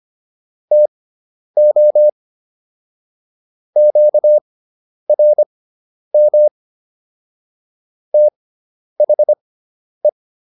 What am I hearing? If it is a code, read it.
Morse code, heard as a keyed tone, about 25 wpm, spaced as for 10 wpm, 600 Hz. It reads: TO QRM THE